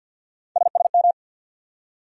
Morse code, heard as a keyed tone, about 50 wpm, 700 Hz. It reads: SSM
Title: USA Call Sign Postfixes - 3 Letters 50wpm